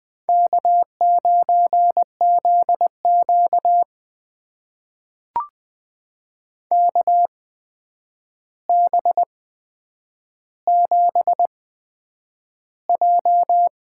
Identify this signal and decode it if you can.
Morse code, heard as a keyed tone, about 20 wpm, spaced as for 9 wpm, 700 Hz. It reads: K9ZQ KB7J
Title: USA Call Signs - 8x Character Spacing - 20wpm